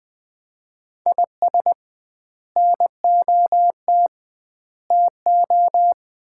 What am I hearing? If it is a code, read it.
Morse code, heard as a keyed tone, about 20 wpm, 700 Hz. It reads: IS NOT TO
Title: 2x Word-Spacing - Sets of 5 Words - 20wpm